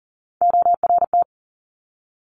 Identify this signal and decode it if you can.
Morse code, heard as a keyed tone, about 38 wpm, 700 Hz. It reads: ORT
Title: Top Three Letter Combinations in Words 40wpm